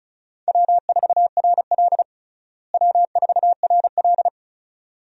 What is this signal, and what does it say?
Morse code, heard as a keyed tone, about 34 wpm, 700 Hz. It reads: W4RL W4RL
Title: USA Call Signs - 4 Characters - Repeated 2x - 35wpm